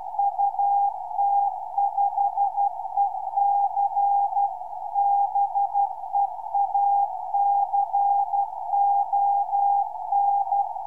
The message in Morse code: UT5EC BEACON